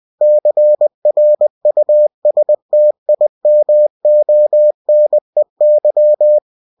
Morse code, heard as a keyed tone, about 20 wpm, 600 Hz. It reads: CRUSTIMONEY
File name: cw-books / winnie-the-pooh / winnie-20wpm-page-0029